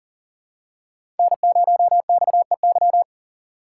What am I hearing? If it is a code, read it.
Morse code, heard as a keyed tone, about 38 wpm, 690 Hz. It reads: N0XEY
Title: USA Call Signs - 4x Character Spacing - 40wpm